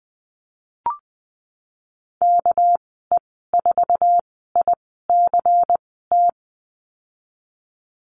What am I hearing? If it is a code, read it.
Morse code, heard as a keyed tone, about 20 wpm, spaced as for 15 wpm, 700 Hz. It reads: KE4ICT
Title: 207 Lesson 33g - Letter j - Call Signs 20wpm